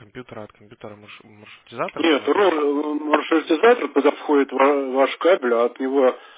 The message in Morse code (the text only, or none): none